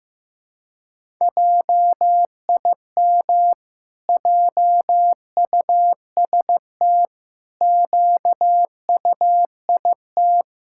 Morse code, heard as a keyed tone, about 16 wpm, 700 Hz. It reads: JIM JUST QUIT
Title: Pangram Challenge v2 - 15wpm